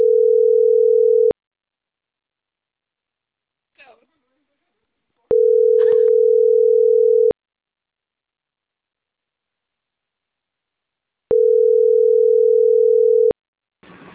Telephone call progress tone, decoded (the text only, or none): ringback tone